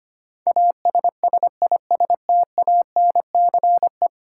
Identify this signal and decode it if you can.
Morse code, heard as a keyed tone, about 25 wpm, 700 Hz. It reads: ASSISTANCE
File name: Ten Letter Words - 25wpm